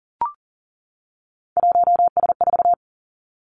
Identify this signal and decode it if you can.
Morse code, heard as a keyed tone, about 41 wpm, 700 Hz. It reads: 1S4
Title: Three Challenging Letter-Number Combinations 40wpm